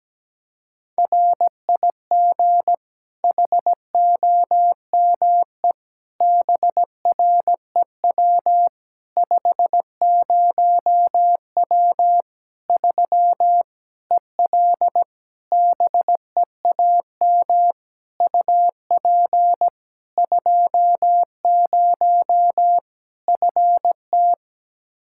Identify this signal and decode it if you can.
Morse code, heard as a keyed tone, about 17 wpm, 700 Hz. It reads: RIG HOME BREW 50W 3 EL BEAM UP 20 FT
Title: Practice QSO - Exchange 2 - Info - 17wpm